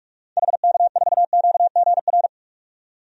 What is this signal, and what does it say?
Morse code, heard as a keyed tone, about 47 wpm, 700 Hz. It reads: HK4QGR
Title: DXCC-100 Call Signs - 6 Characters - 45wpm